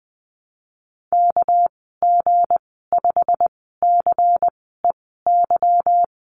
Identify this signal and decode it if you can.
Morse code, heard as a keyed tone, about 20 wpm, spaced as for 15 wpm, 700 Hz. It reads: KG5CEY
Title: 213 Lesson 34g - Number 8 - Call Signs 20wpm